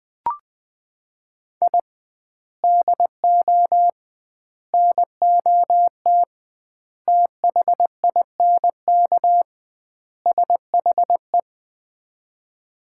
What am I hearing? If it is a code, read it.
Morse code, heard as a keyed tone, about 20 wpm, 700 Hz. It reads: I DO NOT THINK SHE
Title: 2x Word-Spacing - Sets of 5 Words - 20wpm